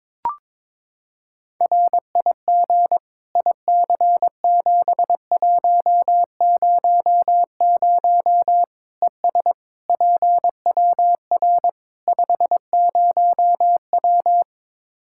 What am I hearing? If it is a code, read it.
Morse code, heard as a keyed tone, about 22 wpm, 700 Hz. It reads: RIG IC7100 ES PWR 50W
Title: Practice QSO - Exchange 2 - Info - 22wpm